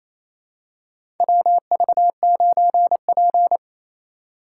Morse code, heard as a keyed tone, about 28 wpm, 700 Hz. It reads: WV9P
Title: USA Call Signs - 4 Characters - 2x Character Spacing - 28wpm